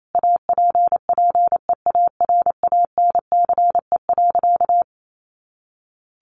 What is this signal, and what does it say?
Morse code, heard as a keyed tone, about 28 wpm, 700 Hz. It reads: APPEARANCE.
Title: Sentences from Top 1100 Words 28wpm